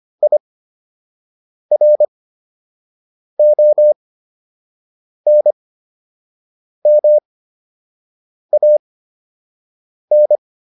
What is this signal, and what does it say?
Morse code, heard as a keyed tone, about 25 wpm, spaced as for 6 wpm, 600 Hz. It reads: IRONMAN